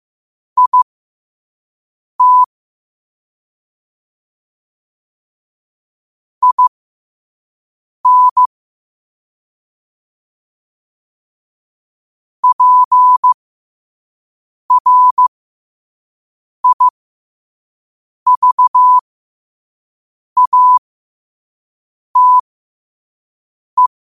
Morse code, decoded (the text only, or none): IT IN PRIVATE